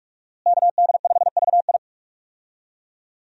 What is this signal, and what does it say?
Morse code, heard as a keyed tone, about 43 wpm, 700 Hz. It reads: KD5VI
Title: USA Call Signs - 5 Characters - Repeated 3x - 45wpm